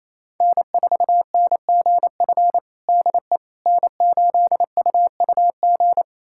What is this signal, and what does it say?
Morse code, heard as a keyed tone, about 28 wpm, 700 Hz. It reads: N4NGF DE N8UUG
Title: USA Callsign DE Callsign - 5 Characters - 28wpm